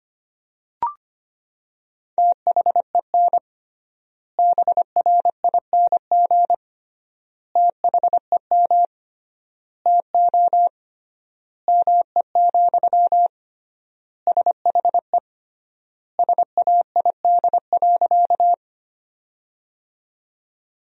Morse code, heard as a keyed tone, about 25 wpm, 700 Hz. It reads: THEN BRING THEM TO ME, SHE SAID.
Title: 3x Word-Spacing - Sentences from Top 500 Words - 25wpm